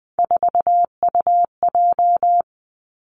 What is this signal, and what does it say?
Morse code, heard as a keyed tone, about 20 wpm, 700 Hz. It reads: 4UJ
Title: Three Challenging Letter-Number Combinations 20wpm